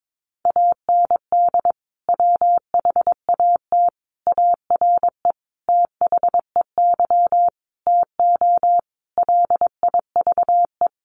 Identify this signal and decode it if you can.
Morse code, heard as a keyed tone, about 22 wpm, 700 Hz. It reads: AND WHAT ARE THEY TO LIVE
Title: Sentences from Top 1100 Words 22wpm